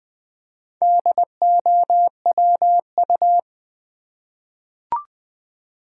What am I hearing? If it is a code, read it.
Morse code, heard as a keyed tone, about 20 wpm, 700 Hz. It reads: DOWU
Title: Four Letter-Number Combinations - 20wpm